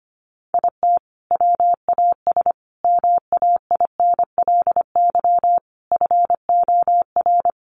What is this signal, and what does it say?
Morse code, heard as a keyed tone, about 25 wpm, 700 Hz. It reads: IT WAS MAINLY FOR
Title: Sentences from Top 1400 Words 25wpm